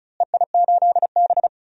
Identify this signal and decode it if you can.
Morse code, heard as a keyed tone, about 34 wpm, 700 Hz. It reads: EI8B
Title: DXCC-100 Call Signs - 4 Characters - Repeated 3x - 35wpm